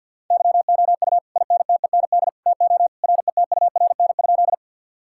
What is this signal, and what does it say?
Morse code, heard as a keyed tone, about 50 wpm, 700 Hz. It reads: YOU INTEND TO RETURN?
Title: Sentences from Top 1600 Words 50wpm